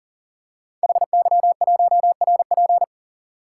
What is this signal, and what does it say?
Morse code, heard as a keyed tone, about 38 wpm, 700 Hz. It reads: HY1RP
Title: Code Groups - 5 Letter:Numbers - 40wpm